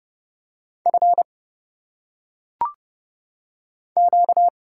Morse code, heard as a keyed tone, about 30 wpm, 700 Hz. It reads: F Q
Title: Single Letters - Phonetic Pronunciation 30wpm